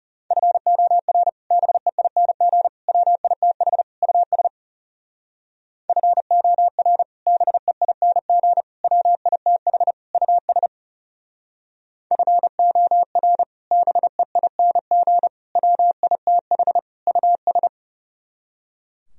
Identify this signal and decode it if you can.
Morse code, heard as a keyed tone, about 37 wpm, 700 Hz. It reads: FOR BEING WITH US FOR BEING WITH US FOR BEING WITH US
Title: Speed-Racing - Sets of 4 Words 30wpm